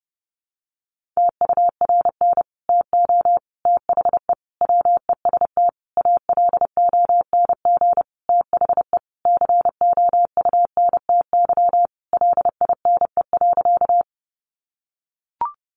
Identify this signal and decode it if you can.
Morse code, heard as a keyed tone, about 30 wpm, 700 Hz. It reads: TURN TO THE WEST ALONG THE COUNTY LINE.
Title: Sentences from Top 1200 Words 30wpm